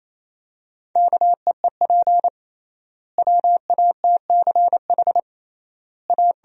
Morse code, heard as a keyed tone, about 28 wpm, 700 Hz. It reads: KEEP WATCH A
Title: 3x Word-Spacing - Sentences from Top 300 Words - 28wpm